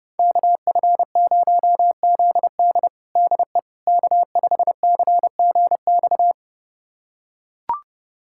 Morse code, heard as a keyed tone, about 30 wpm, 700 Hz. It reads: KF0ZD DE K5CGX
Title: USA Callsign DE Callsign - 5 Characters - 30wpm